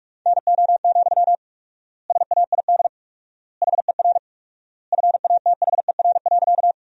Morse code, heard as a keyed tone, about 46 wpm, 700 Hz. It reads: NO, SAID HER FATHER.
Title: 4x Word-Spacing - Sentences from Top 300 Words - 45wpm